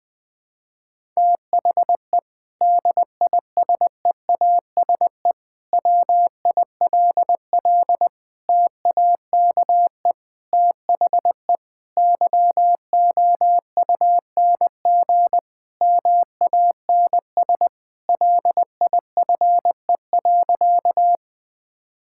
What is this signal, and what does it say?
Morse code, heard as a keyed tone, about 20 wpm, 700 Hz. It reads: THE DISEASE WILL TAKE THE YOUNG MANS LIFE.
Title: Sentences from Top 1000 Words - Review Part 6 - 20wpm